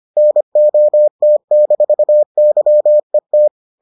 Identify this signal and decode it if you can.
Morse code, heard as a keyed tone, about 25 wpm, 600 Hz. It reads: NOT-YET